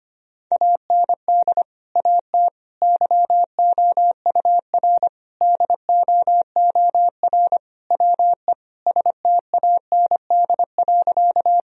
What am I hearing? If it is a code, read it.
Morse code, heard as a keyed tone, about 25 wpm, 700 Hz. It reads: AND AT YOUR DOOR WE STAND.
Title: Sentences from Top 1000 Words - Review Part 1 - 25wpm